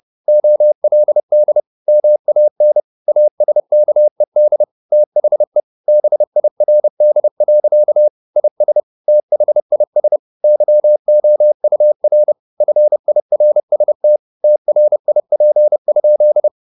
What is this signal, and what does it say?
Morse code, heard as a keyed tone, about 30 wpm, 600 Hz. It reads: OLD MAN ASKED THE BIRD. IS THIS YOUR FIRST TRIP?
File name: the old man and the sea / the old man and the sea 1 30wpm